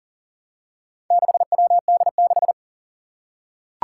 Morse code, heard as a keyed tone, about 41 wpm, 700 Hz. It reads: 6WD6
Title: Four Challenging Letter-Number Combinations 40wpm